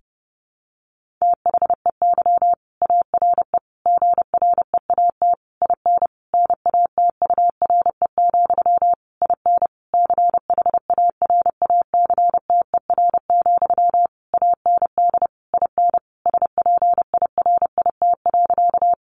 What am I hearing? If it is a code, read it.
Morse code, heard as a keyed tone, about 30 wpm, 700 Hz. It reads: THEY ARE GREAT IN NATURE, IN CHARACTER, AND IN SPIRIT.
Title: Sentences from Top 1400 Words 30wpm